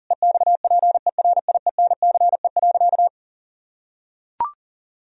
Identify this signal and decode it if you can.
Morse code, heard as a keyed tone, about 40 wpm, 700 Hz. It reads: EXPERIENCE.
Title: Sentences from Top 1000 Words - Review Part 3 - 40wpm